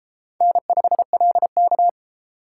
Morse code, heard as a keyed tone, about 31 wpm, 700 Hz. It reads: N5LK